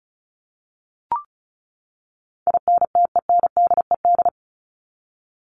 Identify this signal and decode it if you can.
Morse code, heard as a keyed tone, about 34 wpm, 700 Hz. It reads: INTENDED
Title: ED Suffix - Top 100 Words with Base Words 35wpm